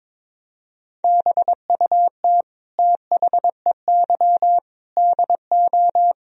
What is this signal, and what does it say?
Morse code, heard as a keyed tone, about 22 wpm, 700 Hz. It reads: BUT THEY DO
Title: Sentences from Top 1000 Words - Review Part 4 - 22wpm